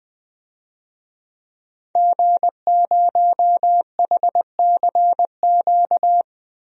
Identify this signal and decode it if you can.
Morse code, heard as a keyed tone, about 20 wpm, 700 Hz. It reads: G0HCQ
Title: DXCC-100 Call Signs - 5 Characters - Repeated 3x - 20wpm